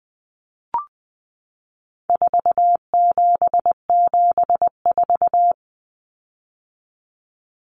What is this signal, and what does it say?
Morse code, heard as a keyed tone, about 20 wpm, 700 Hz. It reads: 4774
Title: Common Cut Numbers - 4-Digit 20wpm